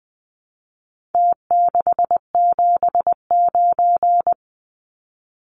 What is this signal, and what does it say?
Morse code, heard as a keyed tone, about 20 wpm, 700 Hz. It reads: T679
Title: Numbers - Contest Style - 4-Digit 20wpm